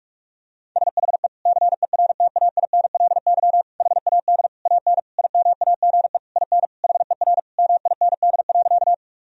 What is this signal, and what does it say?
Morse code, heard as a keyed tone, about 46 wpm, 700 Hz. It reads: SHE CERTAINLY HAD AN IMAGE IN HER MIND.
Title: Sentences from Top 1000 Words - Review Part 4 - 45wpm